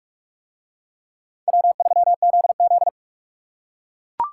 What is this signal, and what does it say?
Morse code, heard as a keyed tone, about 44 wpm, 700 Hz. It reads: W3ZZ